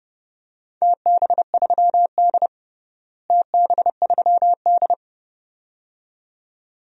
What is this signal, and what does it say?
Morse code, heard as a keyed tone, about 32 wpm, 700 Hz. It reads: TB3D TB3D
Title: DXCC-100 Call Signs - 4 Characters - Repeated 2x - 30wpm